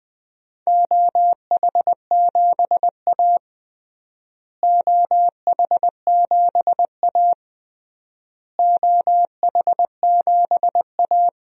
Morse code, heard as a keyed tone, about 20 wpm, 700 Hz. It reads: OH7A OH7A OH7A
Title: DXCC-100 Call Signs - 4 Characters - Repeated 3x - 20wpm